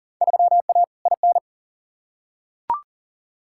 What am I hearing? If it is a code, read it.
Morse code, heard as a keyed tone, about 40 wpm, 700 Hz. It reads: 3A IN
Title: Field Day Exchanges Encore - 40wpm